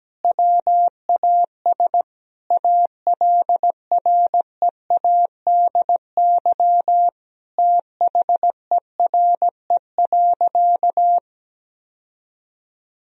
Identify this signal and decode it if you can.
Morse code, heard as a keyed tone, about 17 wpm, 700 Hz. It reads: WAS ALREADY THERE.
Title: Sentences from Top 1000 Words - Review Part 1 - 17wpm